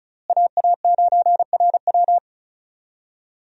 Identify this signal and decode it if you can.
Morse code, heard as a keyed tone, about 34 wpm, 700 Hz. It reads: AA9RW
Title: USA Call Signs - 5 Characters - Repeated 3x - 35wpm